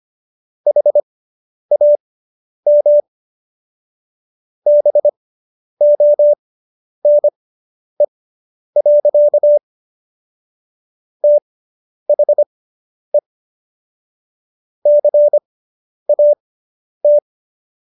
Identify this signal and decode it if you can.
Morse code, heard as a keyed tone, about 25 wpm, spaced as for 10 wpm, 590 Hz. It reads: HAM BONE. THE CAT